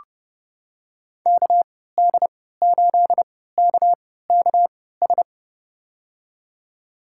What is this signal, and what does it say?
Morse code, heard as a keyed tone, about 30 wpm, spaced as for 17 wpm, 700 Hz. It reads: KD8KKS
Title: USA Call Signs - 6 Characters - 3x Character Spacing - 30wpm